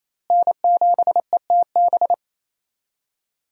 Morse code, heard as a keyed tone, about 28 wpm, 700 Hz. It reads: N7ETB
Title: USA Call Signs - 5 Characters - Repeated 2x - 28wpm